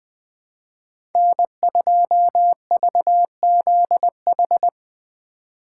Morse code, heard as a keyed tone, about 20 wpm, 700 Hz. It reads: N2VZH
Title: USA Call Signs - 5 Characters - 8x Character Spacing - 20wpm